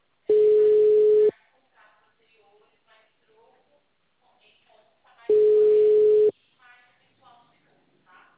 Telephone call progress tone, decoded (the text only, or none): ringback tone